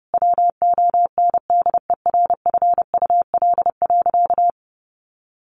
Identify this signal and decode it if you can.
Morse code, heard as a keyed tone, about 30 wpm, 700 Hz. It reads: WONDERFUL.
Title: Sentences from Top 1100 Words 30wpm